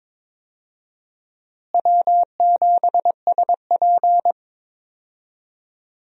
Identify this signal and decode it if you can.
Morse code, heard as a keyed tone, about 22 wpm, 700 Hz. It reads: W7SP
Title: USA Call Signs - 4 Characters - Repeated 3x - 22wpm